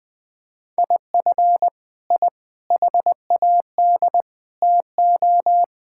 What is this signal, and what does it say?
Morse code, heard as a keyed tone, about 19 wpm, 700 Hz. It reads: IF I HAD TO
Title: Sentences with 9-Word Phrases - Spoken Beginning - 20wpm